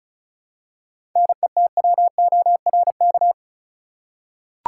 Morse code, heard as a keyed tone, about 34 wpm, 700 Hz. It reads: NETWORK
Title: Seven Letter Words - 35wpm